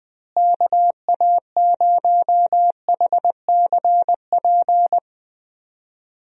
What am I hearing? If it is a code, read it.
Morse code, heard as a keyed tone, about 20 wpm, 700 Hz. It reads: KA0HCP
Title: USA Call Signs - 6 Characters - Repeated 3x - 20wpm